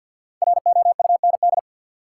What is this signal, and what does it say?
Morse code, heard as a keyed tone, about 46 wpm, 700 Hz. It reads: ROUND
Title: Sets of 5 Words Encore - New Word Drill - 50wpm